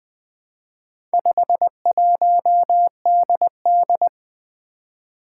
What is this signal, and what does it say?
Morse code, heard as a keyed tone, about 20 wpm, 700 Hz. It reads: 51DD
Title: Four Challenging Letter-Number Combinations 20wpm